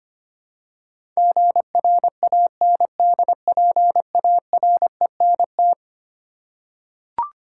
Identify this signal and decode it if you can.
Morse code, heard as a keyed tone, about 25 wpm, 700 Hz. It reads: GRANDPARENT
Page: morse-code-ninja-practice-25wpm